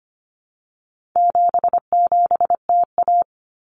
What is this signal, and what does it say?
Morse code, heard as a keyed tone, about 25 wpm, 700 Hz. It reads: 77TA
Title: Common Cut Numbers - 4-Digit 25wpm